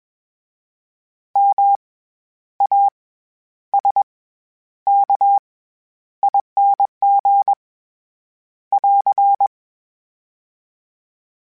Morse code, heard as a keyed tone, about 21 wpm, 800 Hz. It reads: M A S K ING <AR>
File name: Special INGs-mp3s / Special INGs-mp3s / SlowToFaster ing @20